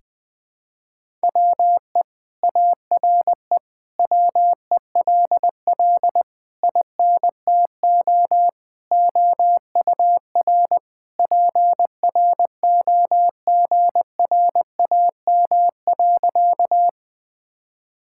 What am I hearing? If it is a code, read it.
Morse code, heard as a keyed tone, about 20 wpm, 700 Hz. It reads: WE ARE WELL INTO OUR PROGRAM.
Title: Sentences from Top 1000 Words - Review Part 5 - 20wpm